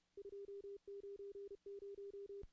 Morse code, heard as a keyed tone, about 33 wpm, 400 Hz. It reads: 290
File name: contests / CW-SS-k5tr- -0000